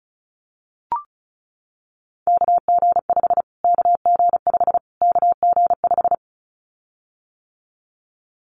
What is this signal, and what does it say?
Morse code, heard as a keyed tone, about 34 wpm, 700 Hz. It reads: KG5 KG5 KG5
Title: Callsign Prefixes for Top 25 DXCC - Repeated 3x - 35wpm